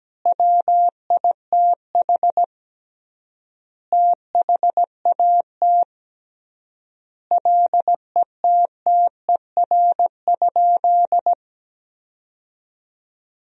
Morse code, heard as a keyed tone, about 17 wpm, 700 Hz. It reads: WITH THAT LETTER?
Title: 3x Word-Spacing - Sentences from Top 300 Words - 17wpm